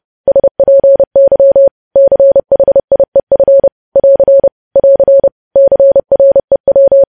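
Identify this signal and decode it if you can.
Morse code, heard as a keyed tone, about 30 wpm, 550 Hz. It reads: SPY CHIEF <AR> <AR> CREW
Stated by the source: News headlines sent in Morse code at 30 wpm.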